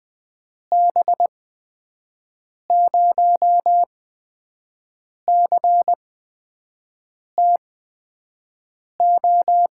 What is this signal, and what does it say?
Morse code, heard as a keyed tone, about 21 wpm, spaced as for 5 wpm, 700 Hz. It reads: B0CTO
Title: USA Call Signs - 6 Characters - 8x Character Spacing - 20wpm